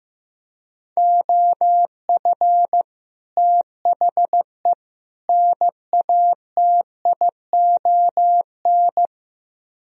Morse code, heard as a keyed tone, about 17 wpm, 700 Hz. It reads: OF THE NATION
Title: Speed-Racing - Sets of 3 Words 15wpm